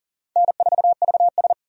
Morse code, heard as a keyed tone, about 40 wpm, 700 Hz. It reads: N4VS
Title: USA Call Signs - 4 Characters - 40wpm